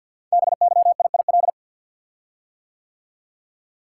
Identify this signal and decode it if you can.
Morse code, heard as a keyed tone, about 48 wpm, 700 Hz. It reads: BYIIL
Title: Code Groups - 5 Letters - 50wpm